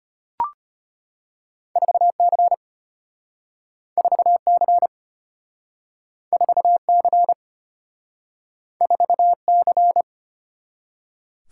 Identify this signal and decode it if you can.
Morse code, heard as a keyed tone, about 35 wpm, 700 Hz. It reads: 4C 4C 4C 4C